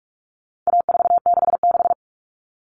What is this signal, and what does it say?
Morse code, heard as a keyed tone, about 44 wpm, 700 Hz. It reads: A466
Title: Common Cut Numbers - 4-Digit 45wpm